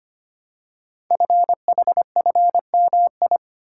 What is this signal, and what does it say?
Morse code, heard as a keyed tone, about 25 wpm, 700 Hz. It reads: FHFMI
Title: Code Groups - 5 Letter:Numbers - 25wpm